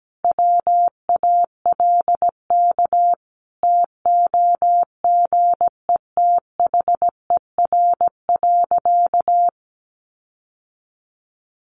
Morse code, heard as a keyed tone, about 17 wpm, 700 Hz. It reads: WALK TOGETHER.
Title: Sentences from Top 1400 Words 17wpm